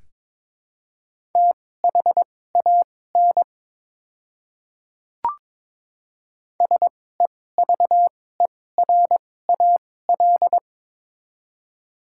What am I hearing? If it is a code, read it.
Morse code, heard as a keyed tone, about 22 wpm, spaced as for 16 wpm, 700 Hz. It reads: THAN SEVERAL
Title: 2x Character Spacing - Top 500 Words - 22wpm